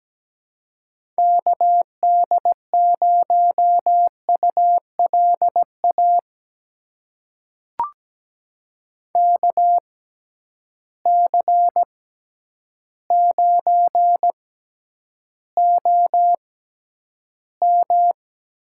Morse code, heard as a keyed tone, about 17 wpm, spaced as for 9 wpm, 700 Hz. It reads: KD0ULA KC9OM
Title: USA Call Signs - 6 Characters - 6x Character Spacing - 17wpm